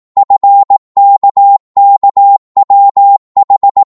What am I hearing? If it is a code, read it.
Morse code, heard as a keyed tone, about 18 wpm, 800 Hz. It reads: FKKWH